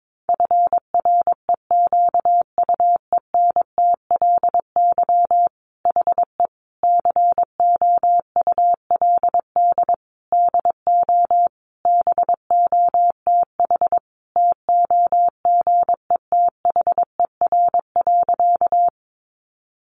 Morse code, heard as a keyed tone, about 22 wpm, 700 Hz. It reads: FREQUENTLY HE COULD DO BOTH TOGETHER.
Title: Sentences from Top 1300 Words 22wpm